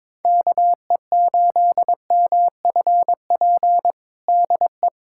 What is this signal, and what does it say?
Morse code, heard as a keyed tone, about 22 wpm, 700 Hz. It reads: KE8MFP DE
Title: USA Callsign DE Callsign - 6 Characters - 22wpm